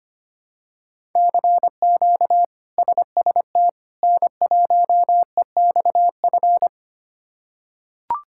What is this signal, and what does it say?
Morse code, heard as a keyed tone, about 25 wpm, 700 Hz. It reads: CQ SST N1EXF
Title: K1USN SST QSO Practice - 25wpm